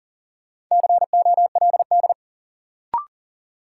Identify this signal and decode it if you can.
Morse code, heard as a keyed tone, about 40 wpm, 700 Hz. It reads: COLD
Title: Sets of 5 Words - New Word Drill - 40wpm